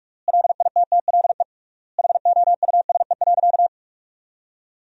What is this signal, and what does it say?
Morse code, heard as a keyed tone, about 42 wpm, 700 Hz. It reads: LITTLE HOUSE.
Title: 3x Word-Spacing - Sentences from Top 200 Words - 45wpm